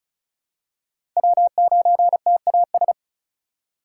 Morse code, heard as a keyed tone, about 34 wpm, 700 Hz. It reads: W9TAS
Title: USA Call Signs - 4x Character Spacing - 35wpm